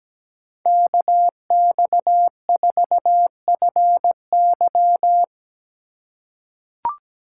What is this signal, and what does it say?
Morse code, heard as a keyed tone, about 17 wpm, 700 Hz. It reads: KX4FY
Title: USA Call Signs - 5 Characters - 3x Character Spacing - 17wpm